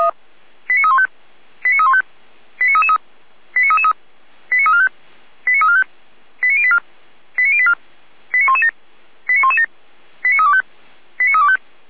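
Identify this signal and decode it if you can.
Five-tone selective call (ZVEI), heard as quick runs of five tones, tones of about 70 ms. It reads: 89315 89315 89333 89333 89345 89345 89084 89084 89117 89117 89325 89325
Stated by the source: Five-tone ZVEI selective call 89230 received by SDR, from audio.f85360.feuerwehr-orlamuende.org (fire brigade Orlamünde).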